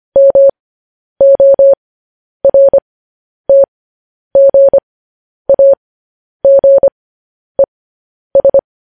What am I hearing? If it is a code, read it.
Morse code, heard as a keyed tone, about 25 wpm, spaced as for 10 wpm, 550 Hz. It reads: MORTGAGES
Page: news headlines in morse code at 10 wpm